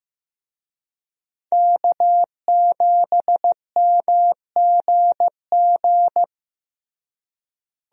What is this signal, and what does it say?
Morse code, heard as a keyed tone, about 15 wpm, 700 Hz. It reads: K7MGG
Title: USA Call Signs - 5 Characters - Repeated 3x - 15wpm